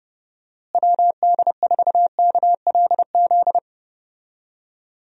Morse code, heard as a keyed tone, about 32 wpm, 700 Hz. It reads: WD4KLZ
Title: USA Call Signs - 6 Characters - Repeated 3x - 30wpm